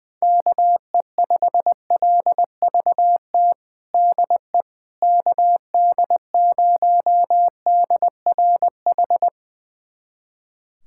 Morse code, heard as a keyed tone, about 21 wpm, 700 Hz. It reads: KE5LVT DE KD0DRH
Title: USA Callsign DE Callsign - 6 Characters - 20wpm